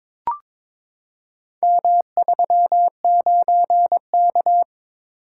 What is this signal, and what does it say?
Morse code, heard as a keyed tone, about 22 wpm, 700 Hz. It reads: M39K
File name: practice / 22wpm / Four Letter-Number Combinations - 22wpm